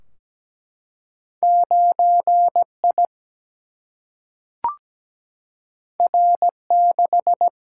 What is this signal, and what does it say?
Morse code, heard as a keyed tone, about 17 wpm, 700 Hz. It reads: 9 I R 6
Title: Single-Letter Single-Number 17wpm